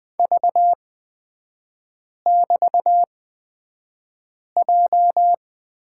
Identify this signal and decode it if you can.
Morse code, heard as a keyed tone, about 20 wpm, spaced as for 5 wpm, 700 Hz. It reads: V<BT>J
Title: 224 Lesson 36e - BT - Three Characters 20wpm